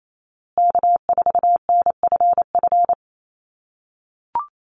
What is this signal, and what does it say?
Morse code, heard as a keyed tone, about 28 wpm, 700 Hz. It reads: K4NFF
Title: USA Call Signs Repeated 2x 28wpm